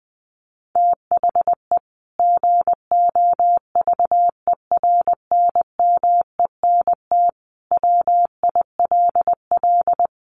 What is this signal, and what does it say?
Morse code, heard as a keyed tone, about 20 wpm, 700 Hz. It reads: THE GOVERNMENT WILL
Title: Sentences from Top 1500 Words 20wpm